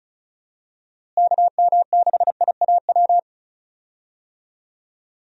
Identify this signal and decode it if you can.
Morse code, heard as a keyed tone, about 36 wpm, 700 Hz. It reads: KM6IAW
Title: USA Call Signs - 6 Characters - Repeated 2x - 35wpm